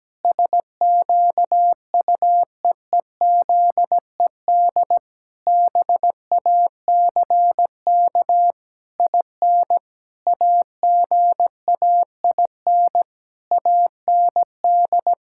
Morse code, heard as a keyed tone, about 17 wpm, 700 Hz. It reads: SQUEEZED BACK IN AGAIN AND
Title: Pangram Challenge v2 - 17wpm